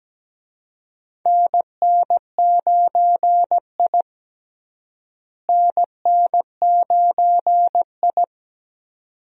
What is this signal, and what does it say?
Morse code, heard as a keyed tone, about 17 wpm, 700 Hz. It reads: NN9I NN9I
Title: USA Call Signs - 4 Characters - Repeated 2x - 17wpm